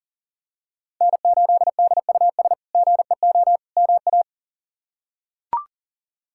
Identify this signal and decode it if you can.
Morse code, heard as a keyed tone, about 40 wpm, 700 Hz. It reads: N8DUS GEO MA
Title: K1USN SST QSO Practice - 40wpm